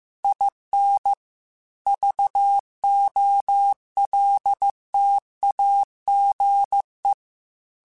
Morse code, heard as a keyed tone, about 15 wpm, 800 Hz. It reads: IN VOLTAGE